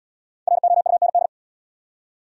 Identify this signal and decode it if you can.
Morse code, heard as a keyed tone, about 68 wpm, 690 Hz. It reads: ROUND